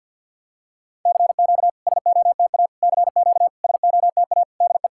Morse code, heard as a keyed tone, about 50 wpm, 700 Hz. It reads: CQ SOTA CQ SOTA DE
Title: SOTA QSO Practice 50wpm